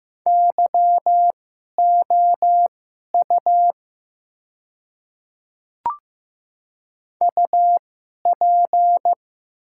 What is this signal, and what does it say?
Morse code, heard as a keyed tone, about 15 wpm, spaced as for 11 wpm, 700 Hz. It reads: YOU UP
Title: 2x Character Spacing - Top 300 Words - 15wpm